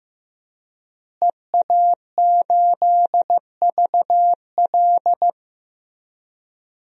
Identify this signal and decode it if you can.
Morse code, heard as a keyed tone, about 15 wpm, 700 Hz. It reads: EA8VL
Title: DXCC-100 Call Signs - 5 Characters - Repeated 3x - 15wpm